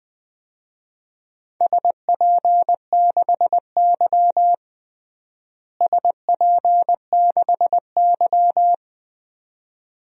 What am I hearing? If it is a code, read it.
Morse code, heard as a keyed tone, about 20 wpm, 700 Hz. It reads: SP6Y SP6Y
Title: DXCC-100 Call Signs - 4 Characters - Repeated 2x - 20wpm